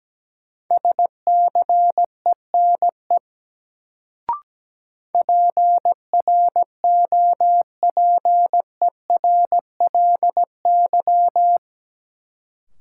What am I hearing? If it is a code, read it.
Morse code, heard as a keyed tone, about 17 wpm, 700 Hz. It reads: SCENE PROPERLY
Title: Sentences from Top 1100 Words - New Word Drill Alt - 17wpm